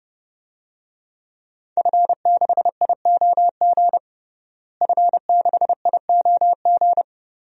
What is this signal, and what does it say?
Morse code, heard as a keyed tone, about 30 wpm, 700 Hz. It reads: F6IOG F6IOG
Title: DXCC-100 Call Signs - 5 Characters - Repeated 2x - 30wpm